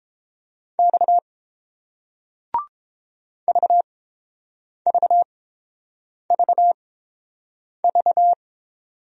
Morse code, heard as a keyed tone, about 30 wpm, spaced as for 7 wpm, 690 Hz. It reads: X VVVV